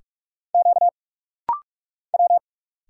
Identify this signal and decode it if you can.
Morse code, heard as a keyed tone, about 45 wpm, 690 Hz. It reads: Q W
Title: Single Letters - Rapid-Fire 45wpm